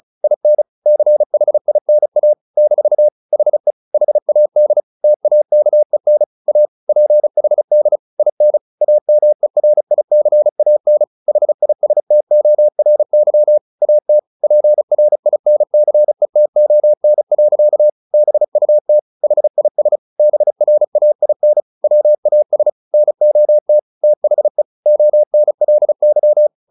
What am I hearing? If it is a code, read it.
Morse code, heard as a keyed tone, about 34 wpm, 600 Hz. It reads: IN CHINA - HE HAD TAKEN A PHD IN AMERICAN HISTORY AT PRINCETON. BUT HIS BRAIN WAS NOT THE ONLY